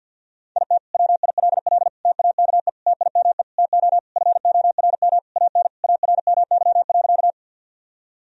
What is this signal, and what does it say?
Morse code, heard as a keyed tone, about 52 wpm, 700 Hz. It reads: IT WILL TAKE TIME TO FORM AN ARMY.
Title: Sentences from Top 1000 Words - Review Part 1 - 50wpm